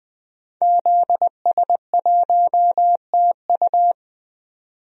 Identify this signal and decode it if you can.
Morse code, heard as a keyed tone, about 20 wpm, 700 Hz. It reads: ZS1TU
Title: DXCC-100 Call Signs - 5 Characters - 20wpm